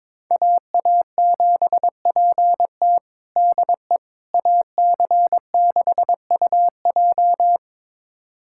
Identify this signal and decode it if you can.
Morse code, heard as a keyed tone, about 21 wpm, 700 Hz. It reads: AA7PT DE AC6UJ
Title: USA Callsign DE Callsign - 5 Characters - 22wpm